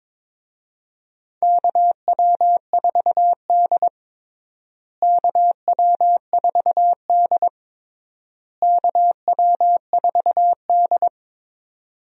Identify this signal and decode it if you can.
Morse code, heard as a keyed tone, about 22 wpm, 700 Hz. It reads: KW4D KW4D KW4D
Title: USA Call Signs - 4 Characters - Repeated 3x - 22wpm